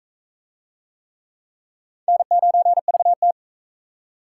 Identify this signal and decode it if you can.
Morse code, heard as a keyed tone, about 41 wpm, 700 Hz. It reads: N9VT